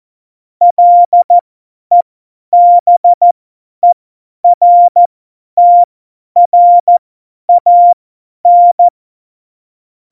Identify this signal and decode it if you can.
Morse code, heard as a keyed tone, about 14 wpm, spaced as for 10 wpm, 700 Hz. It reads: LEBERTRAN
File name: Morsekurs / Witze / Witz 20. Februar 2018, 50 F70